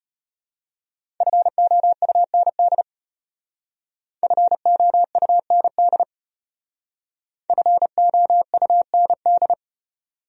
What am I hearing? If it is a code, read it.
Morse code, heard as a keyed tone, about 34 wpm, 700 Hz. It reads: FOUND FOUND FOUND